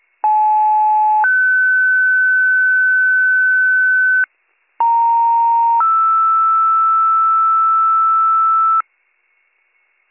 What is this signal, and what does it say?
Two-tone page, heard as a one-second tone, then a three-second tone, about 860 then 1500 Hz, about 930 then 1320 Hz, with noise. It is heard from 0.2 to 8.8 s.